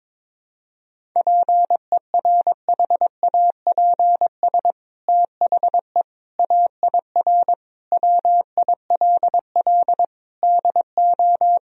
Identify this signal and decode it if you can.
Morse code, heard as a keyed tone, about 22 wpm, 700 Hz. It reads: PERHAPS THE AIR WILL DO
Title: Sentences from Top 1000 Words - Review Part 1 - 22wpm